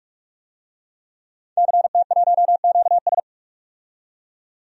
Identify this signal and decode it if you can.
Morse code, heard as a keyed tone, about 45 wpm, 700 Hz. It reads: CT1QS
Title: DXCC-100 Call Signs - 5 Characters - 45wpm